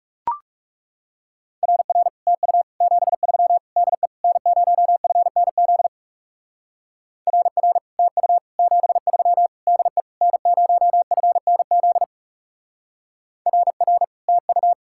Morse code, heard as a keyed tone, about 38 wpm, 700 Hz. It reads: RR TU 73 DE N0FNZ RR TU 73 DE N0FNZ RR TU